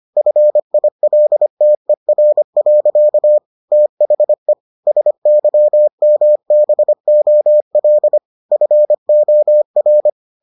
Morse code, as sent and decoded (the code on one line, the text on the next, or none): ..-. .. .-.. - . .-. .-.-.- / - .... / ... -.-- -- -... --- .-.. / ..-. --- .-.
FILTER. THE SYMBOL FOR